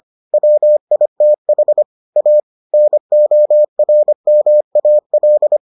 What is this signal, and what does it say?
Morse code, heard as a keyed tone, about 25 wpm, 600 Hz. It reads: WITH A NORMAL